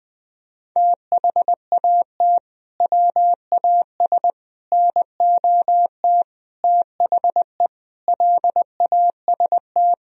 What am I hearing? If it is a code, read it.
Morse code, heard as a keyed tone, about 20 wpm, 700 Hz. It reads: THAT WAS NOT THE LAST
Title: Sentences from Top 1000 Words - Review Part 4 - 20wpm